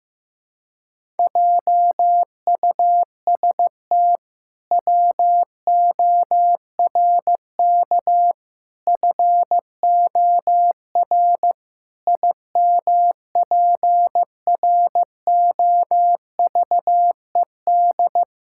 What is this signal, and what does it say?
Morse code, heard as a keyed tone, about 16 wpm, 700 Hz. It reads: JUST WORK FOR IMPROVED
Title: Pangram Challenge v2 - 15wpm